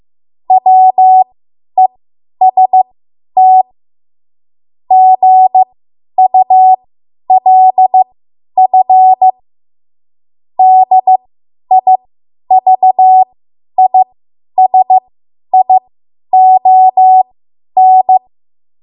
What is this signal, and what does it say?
Morse code, heard as a keyed tone, about 15 wpm, spaced as for 10 wpm, 750 Hz. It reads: WEST GULF DIVISION
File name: Morse / 10WPM